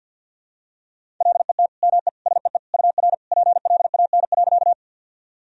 Morse code, heard as a keyed tone, about 52 wpm, 700 Hz. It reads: LET ME SEE UR PLAN.
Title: Sentences from Top 500 Words and Abbreviations 50wpm